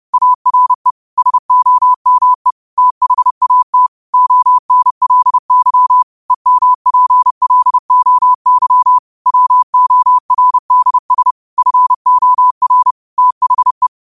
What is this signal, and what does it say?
Morse code, heard as a keyed tone, about 30 wpm, 1000 Hz. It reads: ARE SOME THAT ONLY EMPLOY WORDS FOR THE